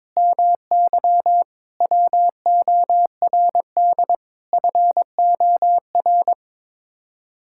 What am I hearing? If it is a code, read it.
Morse code, heard as a keyed tone, about 21 wpm, 700 Hz. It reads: MY WORD FOR
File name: practice / 22wpm / Sets of 8 Words - 22wpm